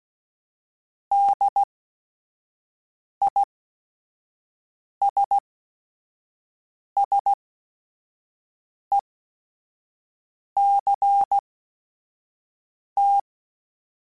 Morse code, heard as a keyed tone, about 16 wpm, spaced as for 5 wpm, 800 Hz. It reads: DISSECT